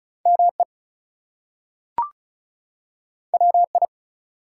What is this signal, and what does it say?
Morse code, heard as a keyed tone, about 36 wpm, 700 Hz. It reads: ME WI
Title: US States and Canadian Provinces 35wpm